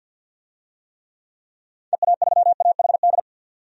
Morse code, heard as a keyed tone, about 47 wpm, 700 Hz. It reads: EA3AHD
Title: DXCC-100 Call Signs - 6 Characters - 50wpm